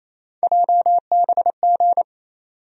Morse code, heard as a keyed tone, about 28 wpm, 700 Hz. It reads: JBG
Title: USA Call Sign Postfixes - 3 Letters 28wpm